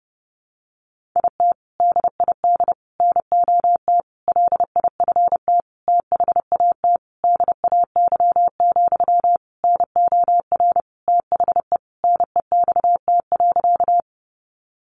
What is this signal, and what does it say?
Morse code, heard as a keyed tone, about 30 wpm, 700 Hz. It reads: IT DID NOT LIFT THAT DAY, NOR THE NEXT.
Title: Sentences from Top 1500 Words 30wpm